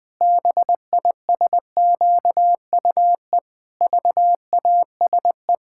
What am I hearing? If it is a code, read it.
Morse code, heard as a keyed tone, about 20 wpm, 700 Hz. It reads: BISQUE VASE
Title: Pangram Challenge v2 - 20wpm